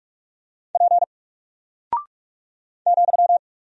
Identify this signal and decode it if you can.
Morse code, heard as a keyed tone, about 44 wpm, 700 Hz. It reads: P ,
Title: Single Character (Letter:Number:Punctuation:Prosign) Alt 45wpm